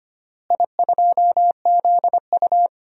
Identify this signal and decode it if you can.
Morse code, heard as a keyed tone, about 25 wpm, 700 Hz. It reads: I2ZU
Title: DXCC-100 Call Signs - 4 Characters - Repeated 3x - 25wpm